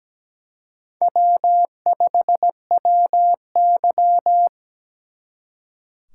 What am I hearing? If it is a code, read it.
Morse code, heard as a keyed tone, about 20 wpm, 700 Hz. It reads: W5WY
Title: Speed Racing - USA 1x2 Call Signs 17wpm